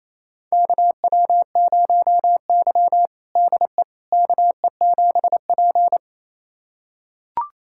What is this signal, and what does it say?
Morse code, heard as a keyed tone, about 28 wpm, 700 Hz. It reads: KW0Y DE KE7P
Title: USA Callsign DE Callsign - 4 Characters - 28wpm